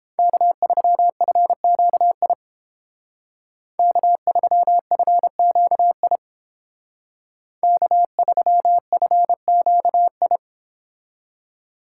Morse code, heard as a keyed tone, about 29 wpm, 700 Hz. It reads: K3FQI K3FQI K3FQI